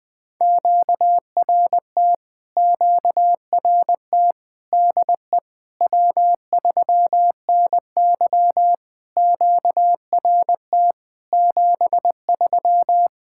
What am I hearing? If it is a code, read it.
Morse code, heard as a keyed tone, about 20 wpm, 700 Hz. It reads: QRT QRT DE W3NY QRT 73
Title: POTA QSO Practice - 20wpm